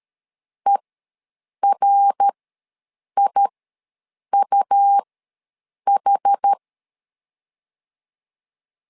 Morse code, heard as a keyed tone, about 13 wpm, spaced as for 7 wpm, 800 Hz. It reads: ERIUH
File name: LEZIONI / 3 MWRO / 3 MWR ws 12wpm 1809